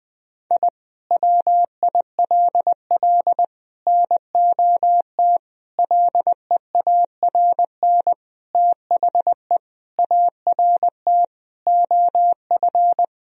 Morse code, heard as a keyed tone, about 20 wpm, 700 Hz. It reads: I WILL NOT LEARN THE ART OF
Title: Sentences from Top 1000 Words - Review Part 3 - 20wpm